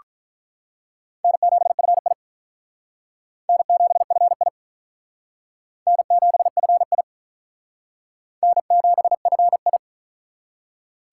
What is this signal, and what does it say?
Morse code, heard as a keyed tone, about 43 wpm, 700 Hz. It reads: N7FI N7FI N7FI N7FI
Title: Speed Racing - USA Call Signs 35wpm